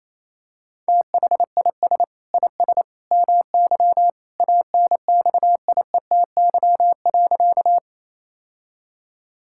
Morse code, heard as a keyed tone, about 27 wpm, 700 Hz. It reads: THIS IS MY ANXIETY.
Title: Sentences from Top 1600 Words 28wpm